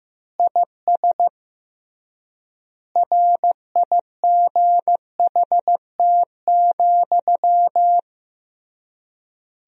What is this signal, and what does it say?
Morse code, heard as a keyed tone, about 15 wpm, 700 Hz. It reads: IS RIGHT,
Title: 3x Word-Spacing - Sentences from Top 200 Words - 15wpm